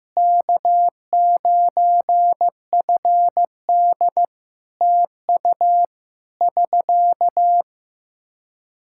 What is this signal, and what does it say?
Morse code, heard as a keyed tone, about 15 wpm, 700 Hz. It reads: K9FD TU <SK>
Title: Practice QSO - Exchange 4 - Ending - 15wpm